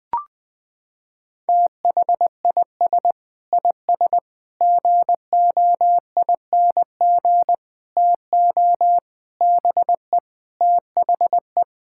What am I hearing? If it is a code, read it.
Morse code, heard as a keyed tone, about 21 wpm, 700 Hz. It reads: THIS IS GOING TO BE THE
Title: Sentences with 4-Word Phrases - 20wpm